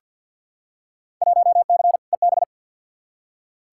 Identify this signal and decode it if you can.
Morse code, heard as a keyed tone, about 46 wpm, 700 Hz. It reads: 1C EB
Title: Field Day Exchanges Encore - 50wpm